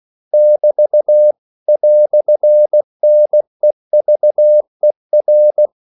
Morse code, heard as a keyed tone, about 16 wpm, 600 Hz. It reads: <BT> "NEVER
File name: CW / Practice / EnglishQuotations0116